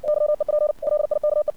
Morse code, heard as a keyed tone, about 57 wpm, 600 Hz. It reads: YEP LIKE
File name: archived / CRSnet